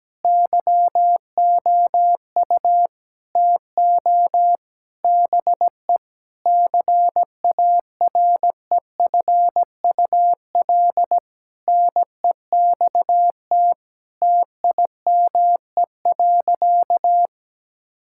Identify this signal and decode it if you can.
Morse code, heard as a keyed tone, about 17 wpm, 700 Hz. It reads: YOU TO BE CAREFUL NEXT TIME.
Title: Sentences from Top 1600 Words 17wpm